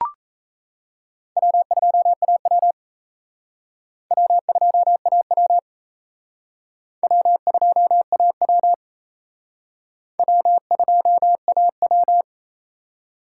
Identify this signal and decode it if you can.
Morse code, heard as a keyed tone, about 35 wpm, 700 Hz. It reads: W2AW W2AW W2AW W2AW
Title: Speed Racing - USA 1x2 Call Signs 28wpm